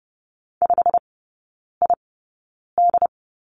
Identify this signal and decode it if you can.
Morse code, heard as a keyed tone, about 30 wpm, spaced as for 9 wpm, 700 Hz. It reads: 5ID
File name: course / 30:10wpm / 087 Lesson 15e - Number 5 - Three Characters 30wpm